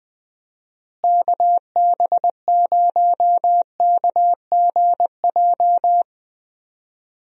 Morse code, heard as a keyed tone, about 19 wpm, 700 Hz. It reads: KB0KGJ
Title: USA Call Signs - 6 Characters - Repeated 3x - 20wpm